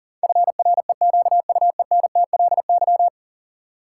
Morse code, heard as a keyed tone, about 40 wpm, 700 Hz. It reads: FREQUENTLY
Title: Sentences from Top 1300 Words - New Word Drill - 40wpm